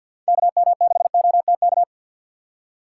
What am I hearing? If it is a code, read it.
Morse code, heard as a keyed tone, about 52 wpm, 700 Hz. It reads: KM6OTK
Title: USA Call Signs - 6 Characters - 6x Character Spacing - 50wpm